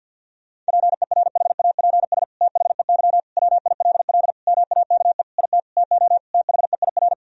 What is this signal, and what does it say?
Morse code, heard as a keyed tone, about 50 wpm, 700 Hz. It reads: PERHAPS THEY WILL MAKE IT TO THEIR
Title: Sentences from Top 1000 Words - Review Part 2 - 50wpm